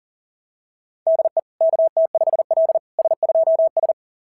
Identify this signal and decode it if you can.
Morse code, heard as a keyed tone, about 40 wpm, 690 Hz. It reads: DE KT5L S2S
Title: SOTA QSO Practice 40wpm